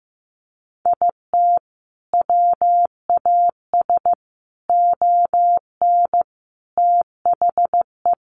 Morse code, heard as a keyed tone, about 15 wpm, 700 Hz. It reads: IT WAS ON THE
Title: Sentences from Top 1200 Words 15wpm